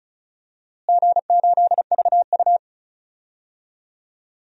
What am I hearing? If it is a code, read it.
Morse code, heard as a keyed tone, about 34 wpm, 700 Hz. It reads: G8VU
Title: DXCC-100 Call Signs - 4 Characters - Repeated 2x - 35wpm